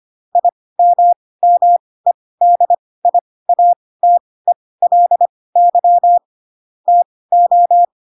Morse code, heard as a keyed tone, about 25 wpm, spaced as for 18 wpm, 700 Hz. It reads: IMMEDIATELY TO
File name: Pinocchio / Pinocchio ch2 12 25 ik1hge